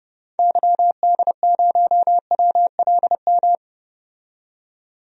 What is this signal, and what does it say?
Morse code, heard as a keyed tone, about 30 wpm, 700 Hz. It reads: YD0WLM
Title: DXCC-100 Call Signs - 6 Characters - 30wpm